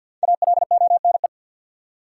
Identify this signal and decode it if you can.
Morse code, heard as a keyed tone, about 45 wpm, 700 Hz. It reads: ALONE
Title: Five Letter Words - 50wpm